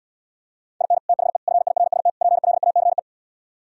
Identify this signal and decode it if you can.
Morse code, heard as a keyed tone, about 77 wpm, 700 Hz. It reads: IN THE FIRST PLACE